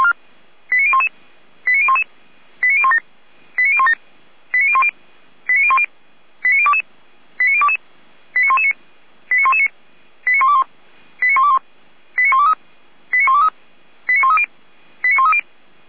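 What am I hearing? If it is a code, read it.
Five-tone selective call (ZVEI), heard as quick runs of five tones, tones of about 70 ms. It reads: 89011 89011 89017 89017 89010 89010 89022 89022 89119 89119 89121 89121 89123 89123 89130 89130
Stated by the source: ZVEI five-tone call received via SDR, archived by the Feuerwehr Orlamünde fire brigade.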